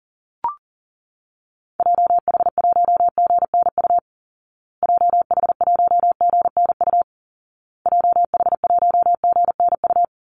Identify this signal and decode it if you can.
Morse code, heard as a keyed tone, about 40 wpm, 700 Hz. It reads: JH1GNU JH1GNU JH1GNU
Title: Call Signs Repeated 2x - 40wpm